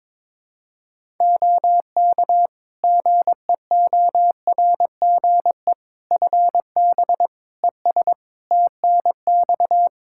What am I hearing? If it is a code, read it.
Morse code, heard as a keyed tone, about 21 wpm, 700 Hz. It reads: OK GEORGE FB ES TNX
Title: K1USN SST QSO Practice 22wpm